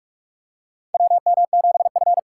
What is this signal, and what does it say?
Morse code, heard as a keyed tone, about 44 wpm, 700 Hz. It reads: WM7F
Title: USA Call Signs - 4 Characters - 6x Character Spacing - 45wpm